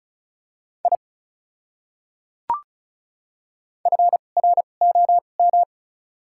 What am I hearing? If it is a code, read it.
Morse code, heard as a keyed tone, about 36 wpm, spaced as for 25 wpm, 700 Hz. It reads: I FROM
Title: 2x Character Spacing - Top 300 Words - 35wpm